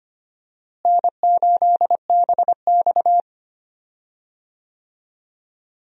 Morse code, heard as a keyed tone, about 25 wpm, 700 Hz. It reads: N8BX
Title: USA Call Signs - 4 Characters - Repeated 3x - 25wpm